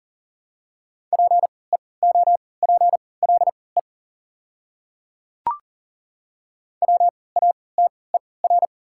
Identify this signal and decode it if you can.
Morse code, heard as a keyed tone, about 37 wpm, spaced as for 22 wpm, 690 Hz. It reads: PEOPLE WATER
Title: 3x Character Spacing - Top 100 Words - 40wpm